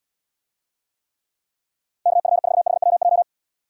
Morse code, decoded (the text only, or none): KF4HWJ